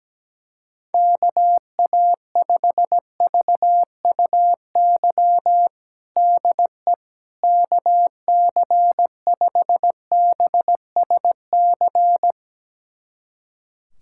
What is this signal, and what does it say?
Morse code, heard as a keyed tone, about 17 wpm, 700 Hz. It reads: KA5VUY DE KC5BSC
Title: USA Callsign DE Callsign - 6 Characters - 17wpm